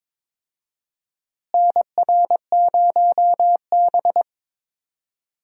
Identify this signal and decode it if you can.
Morse code, heard as a keyed tone, about 22 wpm, 700 Hz. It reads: NR0B
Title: USA Call Signs - 4 Characters - Repeated 3x - 22wpm